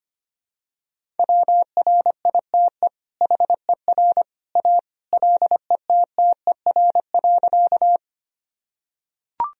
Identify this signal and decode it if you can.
Morse code, heard as a keyed tone, about 25 wpm, 700 Hz. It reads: WRITE HER A LETTER.
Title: Sentences from Top 1000 Words - Review Part 1 - 25wpm